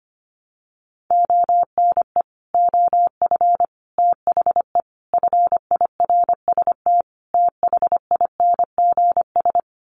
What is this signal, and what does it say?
Morse code, heard as a keyed tone, about 25 wpm, 700 Hz. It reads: ONE OF THE FIRST THINGS
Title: Sets of 5 Words Encore 25wpm